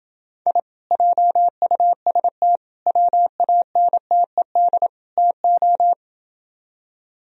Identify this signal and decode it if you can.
Morse code, heard as a keyed tone, about 26 wpm, 700 Hz. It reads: I JUST WANTED TO